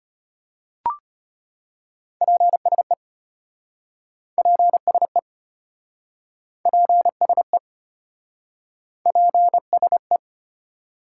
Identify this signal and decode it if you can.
Morse code, heard as a keyed tone, about 34 wpm, 700 Hz. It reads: PSE PSE PSE PSE